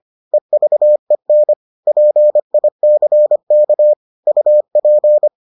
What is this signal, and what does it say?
Morse code, heard as a keyed tone, about 25 wpm, 600 Hz. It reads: EVEN PICK UP